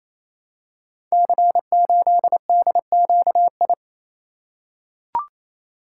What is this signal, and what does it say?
Morse code, heard as a keyed tone, about 28 wpm, 700 Hz. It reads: C8DQI
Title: Code Groups - 5 Letter:Numbers - 28wpm